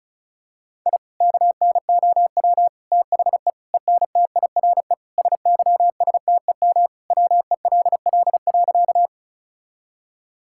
Morse code, heard as a keyed tone, about 36 wpm, 700 Hz. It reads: I KNOW THE ENTIRE SYSTEM WELL.
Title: Sentences from Top 1000 Words - Review Part 5 - 35wpm